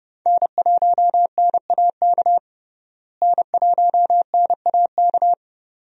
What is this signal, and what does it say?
Morse code, heard as a keyed tone, about 30 wpm, 700 Hz. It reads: N1NAK N1NAK
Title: USA Call Signs - 5 Characters - Repeated 2x - 30wpm